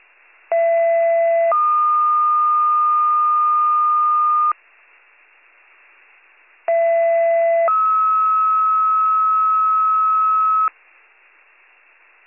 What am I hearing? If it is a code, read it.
Two-tone page, heard as a one-second tone, then a three-second tone, about 670 then 1220 Hz, about 670 then 1290 Hz. It is heard from 0.5 to 10.7 s.